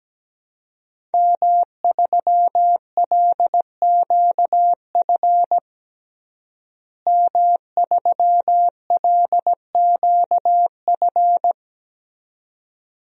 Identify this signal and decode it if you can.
Morse code, heard as a keyed tone, about 17 wpm, 700 Hz. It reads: M3LQF M3LQF
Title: DXCC-100 Call Signs - 5 Characters - Repeated 2x - 17wpm